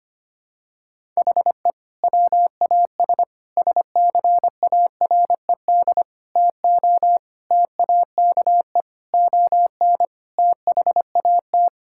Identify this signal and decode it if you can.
Morse code, heard as a keyed tone, about 25 wpm, 700 Hz. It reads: HE WAS SCARED TO TAKE ON THAT
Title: Sentences from Top 1000 Words - Review Part 6 - 25wpm